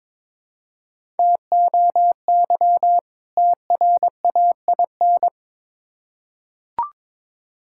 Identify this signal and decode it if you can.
Morse code, heard as a keyed tone, about 22 wpm, 700 Hz. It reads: TOY TRAIN
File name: Difficult Words with Hints - 22wpm